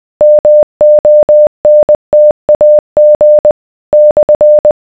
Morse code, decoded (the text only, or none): MONTAG /